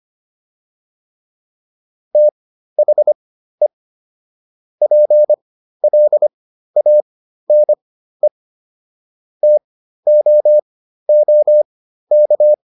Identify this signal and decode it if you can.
Morse code, heard as a keyed tone, about 25 wpm, spaced as for 13 wpm, 600 Hz. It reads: THE PLANE TOOK